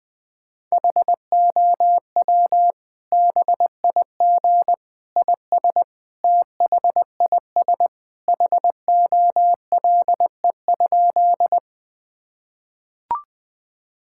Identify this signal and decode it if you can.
Morse code, heard as a keyed tone, about 21 wpm, 700 Hz. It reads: HOW BIG IS THIS HOLE?
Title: Sentences from Top 1600 Words 20wpm